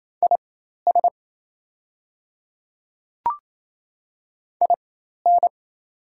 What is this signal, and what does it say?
Morse code, heard as a keyed tone, about 28 wpm, spaced as for 13 wpm, 700 Hz. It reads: IS IN
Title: 4x Character Spacing - Top 100 Words - 28wpm